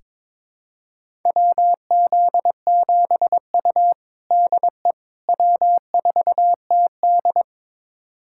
Morse code, heard as a keyed tone, about 21 wpm, 700 Hz. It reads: WZ7U DE W4TD
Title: USA Callsign DE Callsign - 4 Characters - 22wpm